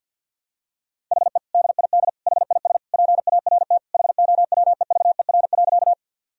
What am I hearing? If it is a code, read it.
Morse code, heard as a keyed tone, about 53 wpm, 700 Hz. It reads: HE DID HIS PART HOWEVER.
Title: Sentences from Top 1000 Words - Review Part 1 - 50wpm